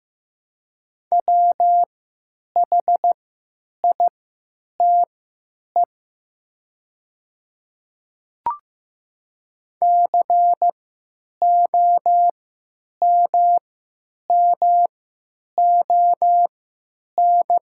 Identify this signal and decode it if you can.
Morse code, heard as a keyed tone, about 15 wpm, spaced as for 9 wpm, 700 Hz. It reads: WHITE COMMON
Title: 3x Character Spacing - Top 500 Words - 15wpm